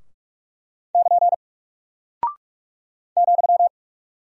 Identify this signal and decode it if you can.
Morse code, heard as a keyed tone, about 46 wpm, 700 Hz. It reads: <KN> ,
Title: Single Character (Letter:Number:Punctuation:Prosign) Alt 45wpm